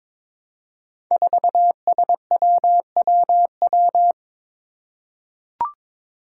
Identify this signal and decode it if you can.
Morse code, heard as a keyed tone, about 22 wpm, 700 Hz. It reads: 4SWWW
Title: Five Challenging Letter-Number Combinations 22wpm